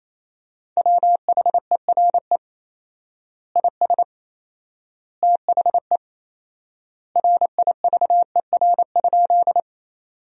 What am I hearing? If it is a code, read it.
Morse code, heard as a keyed tone, about 28 wpm, 700 Hz. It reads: WHERE IS THE RIVER?
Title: 4x Word-Spacing - Sentences from Top 300 Words - 28wpm